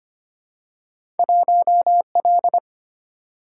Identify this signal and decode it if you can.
Morse code, heard as a keyed tone, about 25 wpm, 700 Hz. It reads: 1L
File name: Two Letter-Number Combinations - 25wpm